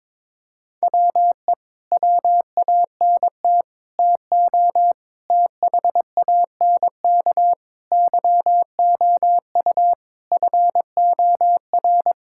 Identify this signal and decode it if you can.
Morse code, heard as a keyed tone, about 21 wpm, 700 Hz. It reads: WE WANT TO THANK YOU FOR
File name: Sets of 9 Words - 22wpm